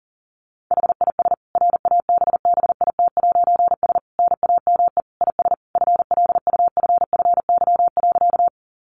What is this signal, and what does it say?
Morse code, heard as a keyed tone, about 40 wpm, 700 Hz. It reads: HIS RABBIT'S NAME IS FLUFFY.
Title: Easy Phrases 40wpm